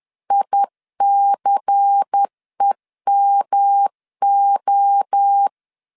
Morse code, heard as a keyed tone, about 10 wpm, 800 Hz. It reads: ICEMO
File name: LEZIONI / 6 PKX0 / 6 PKX0 10wpm 2707